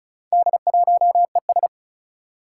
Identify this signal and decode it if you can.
Morse code, heard as a keyed tone, about 35 wpm, 700 Hz. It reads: D1ES
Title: Four Challenging Letter-Number Combinations 35wpm